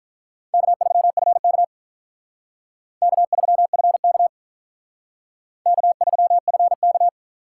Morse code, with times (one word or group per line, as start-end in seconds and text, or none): 0.5-1.6 K3FK
3.0-4.3 K3FK
5.7-7.1 K3FK